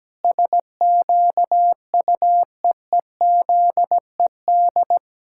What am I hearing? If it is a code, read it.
Morse code, heard as a keyed tone, about 17 wpm, 700 Hz. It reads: SQUEEZED
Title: Pangram Challenge v2 - 17wpm